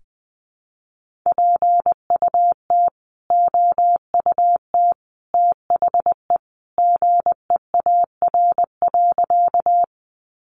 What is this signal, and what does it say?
Morse code, heard as a keyed tone, about 20 wpm, 700 Hz. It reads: PUT OUT THE GEAR.
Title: Sentences from Top 1400 Words 20wpm